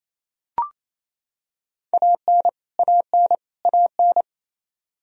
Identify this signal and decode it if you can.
Morse code, heard as a keyed tone, about 28 wpm, 700 Hz. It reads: AN AN AN
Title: Top 100 Words Repeated 28wpm